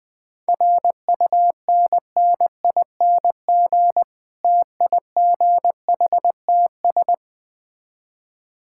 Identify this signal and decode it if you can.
Morse code, heard as a keyed tone, about 20 wpm, 700 Hz. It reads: RUNNING TIGHTS
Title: Difficult Words with Hints - 20wpm